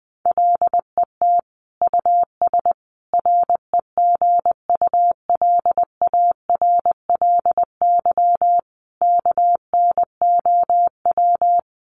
Morse code, heard as a keyed tone, about 20 wpm, 700 Hz. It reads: LET US REGULARLY KNOW
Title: Sentences from Top 1400 Words 20wpm